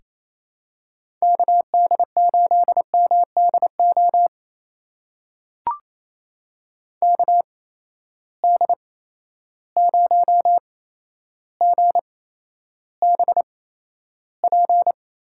Morse code, heard as a keyed tone, about 29 wpm, spaced as for 12 wpm, 700 Hz. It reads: KD8MDO KD0GBP
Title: USA Call Signs - 6 Characters - 8x Character Spacing - 28wpm